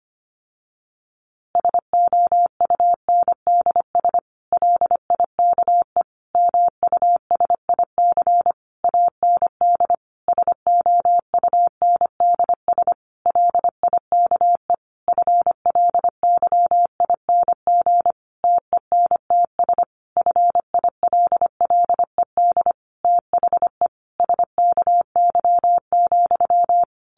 Morse code, heard as a keyed tone, about 25 wpm, 700 Hz. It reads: SOUNDS LIKE MUSIC AND SOUNDS LIKE FLYING TENTS FILLED THE SKY,
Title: A Sound Of Thunder 25wpm